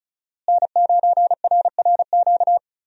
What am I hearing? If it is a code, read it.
Morse code, heard as a keyed tone, about 34 wpm, 700 Hz. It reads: N9RRQ
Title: USA Call Signs - 5 Characters - Repeated 2x - 35wpm